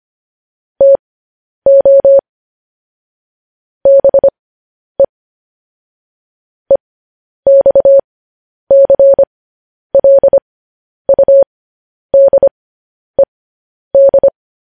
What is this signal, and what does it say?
Morse code, heard as a keyed tone, about 25 wpm, spaced as for 10 wpm, 550 Hz. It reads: TO BE EXCLUDED
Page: news headlines in morse code at 10 wpm